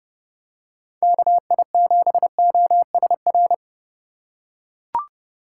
Morse code, heard as a keyed tone, about 30 wpm, 700 Hz. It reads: KI7OSR
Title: USA Call Signs - 6 Characters - 4x Character Spacing - 30wpm